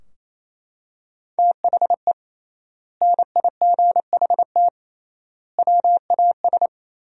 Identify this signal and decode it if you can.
Morse code, heard as a keyed tone, about 28 wpm, 700 Hz. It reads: THE NIGHT WAS
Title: 3x Word-Spacing - Sentences from Top 500 Words - 28wpm